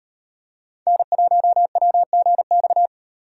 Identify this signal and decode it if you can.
Morse code, heard as a keyed tone, about 38 wpm, 700 Hz. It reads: N1WGX